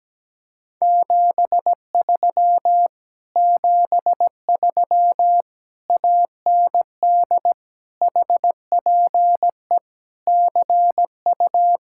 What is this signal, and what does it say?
Morse code, heard as a keyed tone, about 17 wpm, 700 Hz. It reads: 73 73 AND HPE CU
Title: Speed-Racing Common QSO Phrases Encore - 17wpm